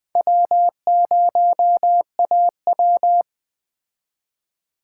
Morse code, heard as a keyed tone, about 20 wpm, 700 Hz. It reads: W0AW
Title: Speed Racing - USA Call Signs 17wpm